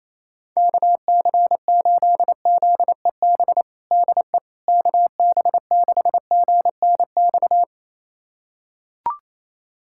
Morse code, heard as a keyed tone, about 27 wpm, 700 Hz. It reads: KC8ZEB DE KB6GNX
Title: USA Callsign DE Callsign 28wpm